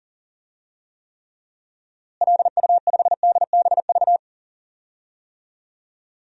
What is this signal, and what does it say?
Morse code, heard as a keyed tone, about 40 wpm, 700 Hz. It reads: LU5DBV
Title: DXCC-100 Call Signs - 6 Characters - Repeated 3x - 40wpm